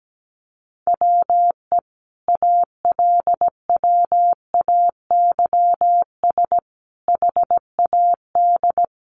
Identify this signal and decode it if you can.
Morse code, heard as a keyed tone, about 17 wpm, 700 Hz. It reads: WE ALWAYS HAD
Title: Sentences from Top 1300 Words 17wpm